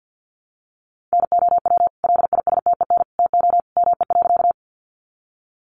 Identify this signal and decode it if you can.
Morse code, heard as a keyed tone, about 50 wpm, 700 Hz. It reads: NOW LISTEN TO ME.
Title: Sentences from Top 400 Words 50wpm